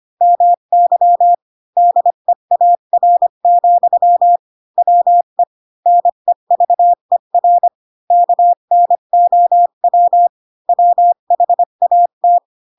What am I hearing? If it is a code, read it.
Morse code, heard as a keyed tone, about 25 wpm, 700 Hz. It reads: MY DEAR, WE NEVER KNOW WHAT
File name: Pinocchio / Pinocchio ch7 20 25 ik1hge